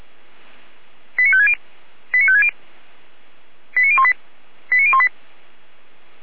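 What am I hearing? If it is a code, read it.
Five-tone selective call (ZVEI), heard as quick runs of five tones, tones of about 70 ms. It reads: 89560 89560 89018 89018
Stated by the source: ZVEI five-tone call 89391 received with an SDR.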